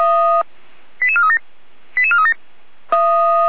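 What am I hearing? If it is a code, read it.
Five-tone selective call (ZVEI), heard as quick runs of five tones, tones of about 70 ms. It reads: 88437 88437
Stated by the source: ZVEI five-tone selective call received with an SDR, from the Feuerwehr Orlamünde fire brigade audio server.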